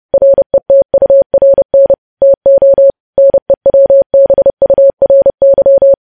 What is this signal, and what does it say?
Morse code, heard as a keyed tone, about 30 wpm, 550 Hz. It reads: RETURN TO NEWBURY
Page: news headlines in morse code at 30 wpm